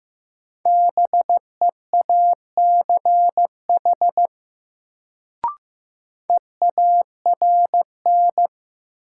Morse code, heard as a keyed tone, about 15 wpm, 700 Hz. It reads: BEACH EARN
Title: Sentences from Top 1500 Words - New Word Drill Alt - 15wpm